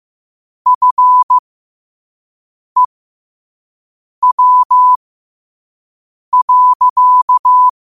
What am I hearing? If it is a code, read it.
Morse code, heard as a keyed tone, about 15 wpm, spaced as for 5 wpm, 1000 Hz. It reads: FEW.
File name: qotdfiles / 5wpm